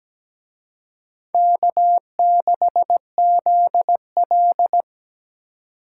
Morse code, heard as a keyed tone, about 20 wpm, 700 Hz. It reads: K6ZL
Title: Speed Racing - USA 1x2 Call Signs 17wpm